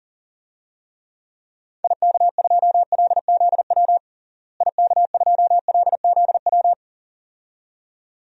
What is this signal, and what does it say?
Morse code, heard as a keyed tone, about 40 wpm, 700 Hz. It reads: IK2LZW IK2LZW
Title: DXCC-100 Call Signs - 6 Characters - Repeated 2x - 40wpm